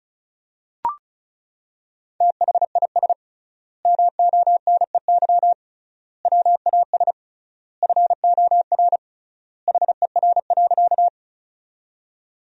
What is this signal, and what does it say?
Morse code, heard as a keyed tone, about 34 wpm, 700 Hz. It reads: THIS MONEY WAS FOR HER.
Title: 3x Word-Spacing - Sentences from Top 400 Words - 35wpm